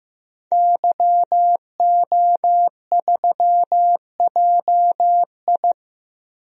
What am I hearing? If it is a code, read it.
Morse code, heard as a keyed tone, about 16 wpm, 700 Hz. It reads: YO3JI
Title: DXCC-100 Call Signs - 5 Characters - 15wpm